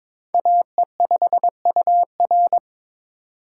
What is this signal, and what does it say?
Morse code, heard as a keyed tone, about 22 wpm, 700 Hz. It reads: AE5UR
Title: USA Call Signs - 5 Characters - 3x Character Spacing - 22wpm